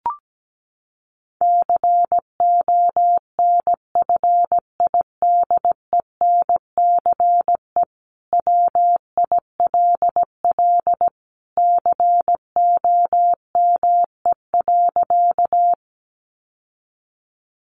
Morse code, heard as a keyed tone, about 17 wpm, 700 Hz. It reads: CONFIDENCE WILL COME.
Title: Sentences from Top 1500 Words 17wpm